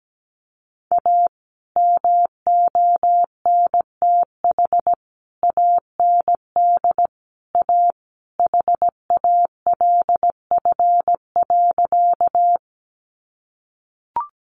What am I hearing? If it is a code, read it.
Morse code, heard as a keyed tone, about 17 wpm, 700 Hz. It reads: A MONTH AND A HALF.
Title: Sentences from Top 800 Words 17wpm